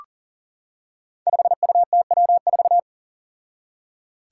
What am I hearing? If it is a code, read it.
Morse code, heard as a keyed tone, about 40 wpm, 700 Hz. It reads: 5UTW4
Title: Five Challenging Letter-Number Combinations 40wpm